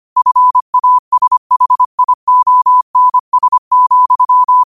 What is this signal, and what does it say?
Morse code, heard as a keyed tone, about 25 wpm, 1000 Hz. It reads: FASHIONS,
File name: qotdfiles / 25wpm